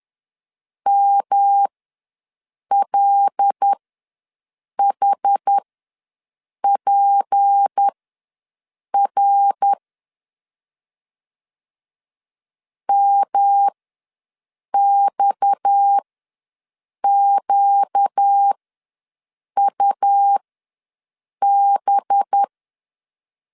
Morse code, heard as a keyed tone, about 10 wpm, spaced as for 6 wpm, 800 Hz. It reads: MLHPR MXQUB
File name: LEZIONI / 6 PKX0 / 6 PKX ws 10wpm 3706